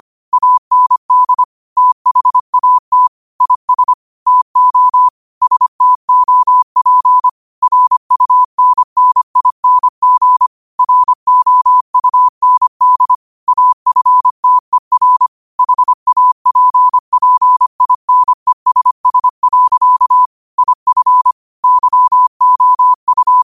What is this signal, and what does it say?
Morse code, heard as a keyed tone, about 25 wpm, 1000 Hz. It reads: AND THAT IS TO STOP RUNNING ROUND AFTER HAPPINESS. IF YOU
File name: qotdfiles / 25wpm